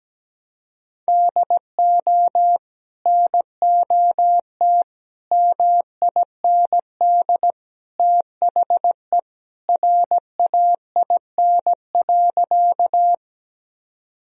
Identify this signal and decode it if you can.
Morse code, heard as a keyed tone, about 17 wpm, 700 Hz. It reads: DO NOT MIND THE RAIN.
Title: Sentences from Top 1000 Words - Review Part 1 - 17wpm